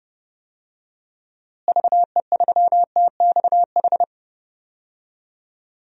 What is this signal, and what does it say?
Morse code, heard as a keyed tone, about 32 wpm, 700 Hz. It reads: VE3TXH
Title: DXCC-100 Call Signs - 6 Characters - Repeated 3x - 30wpm